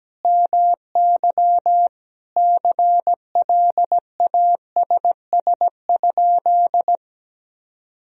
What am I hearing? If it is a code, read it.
Morse code, heard as a keyed tone, about 17 wpm, 700 Hz. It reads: MY CLASS?
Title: Sentences from Top 1000 Words - Review Part 2 - 17wpm